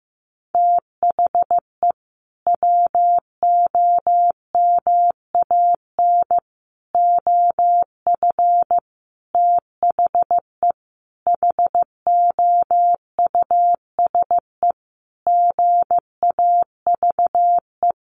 Morse code, heard as a keyed tone, about 15 wpm, 700 Hz. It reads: THE WOMAN OF THE HOUSE GAVE
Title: Sentences from Top 1500 Words 15wpm